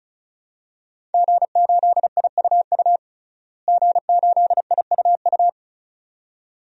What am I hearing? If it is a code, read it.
Morse code, heard as a keyed tone, about 36 wpm, 700 Hz. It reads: G8IUU G8IUU
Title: DXCC-100 Call Signs - 5 Characters - Repeated 2x - 35wpm